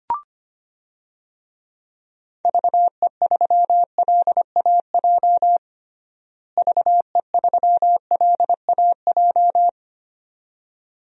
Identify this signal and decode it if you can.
Morse code, heard as a keyed tone, about 25 wpm, 700 Hz. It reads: VE3LAJ VE3LAJ
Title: DXCC-100 Call Signs - 6 Characters - Repeated 2x - 25wpm